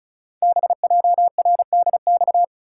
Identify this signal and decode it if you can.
Morse code, heard as a keyed tone, about 36 wpm, 700 Hz. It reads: BJRDX
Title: Code Groups - 5 Letters - 35wpm